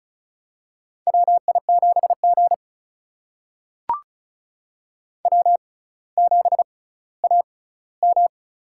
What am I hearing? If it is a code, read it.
Morse code, heard as a keyed tone, about 36 wpm, spaced as for 18 wpm, 700 Hz. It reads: WI7G W7AM
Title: USA Call Signs - 4 Characters - 6x Character Spacing - 35wpm